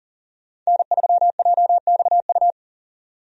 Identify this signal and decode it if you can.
Morse code, heard as a keyed tone, about 38 wpm, 700 Hz. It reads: N3JXU
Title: USA Call Signs - 5 Characters - 40wpm